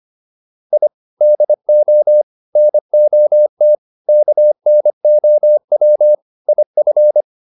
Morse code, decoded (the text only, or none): I DO NOT KNOW IF